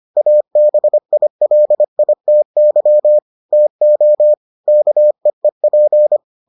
Morse code, heard as a keyed tone, about 25 wpm, 600 Hz. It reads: ABILITY TO KEEP